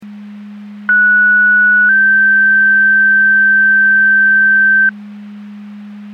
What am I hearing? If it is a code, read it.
Two-tone page, heard as a one-second tone, then a three-second tone, about 1470 then 1550 Hz. It is heard from 0.9 to 4.9 s.
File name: Portage County 154.3850 Fire EMS Paging